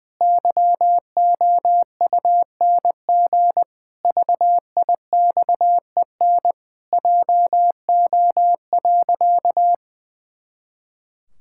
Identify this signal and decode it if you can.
Morse code, heard as a keyed tone, about 20 wpm, 700 Hz. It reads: YOUNG VIXEN JO.
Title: Pangram Challenge v2 - 20wpm